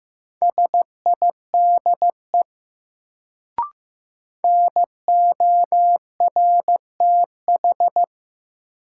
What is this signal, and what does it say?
Morse code, heard as a keyed tone, about 15 wpm, 700 Hz. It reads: SIDE NORTH
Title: Sentences from Top 400 Words - New Word Drill - 15wpm